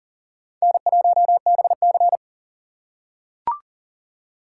Morse code, heard as a keyed tone, about 40 wpm, 700 Hz. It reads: N1BC
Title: USA Call Signs - 4 Characters - 3x Character Spacing - 40wpm